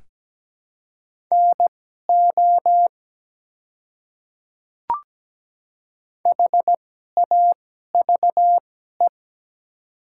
Morse code, heard as a keyed tone, about 17 wpm, spaced as for 12 wpm, 700 Hz. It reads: NO HAVE
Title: 2x Character Spacing - Top 100 Words - 17wpm